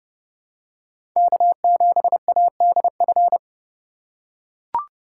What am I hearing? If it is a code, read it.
Morse code, heard as a keyed tone, about 32 wpm, 700 Hz. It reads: K7ADF
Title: USA Call Signs - 5 Characters - 4x Character Spacing - 30wpm